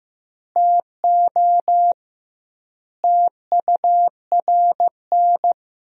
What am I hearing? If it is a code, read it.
Morse code, heard as a keyed tone, about 15 wpm, 700 Hz. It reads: TO TURN
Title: 2x Word-Spacing - Sentences from Top 200 Words - 15wpm